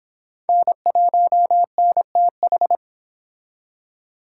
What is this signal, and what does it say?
Morse code, heard as a keyed tone, about 25 wpm, 700 Hz. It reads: N1NTH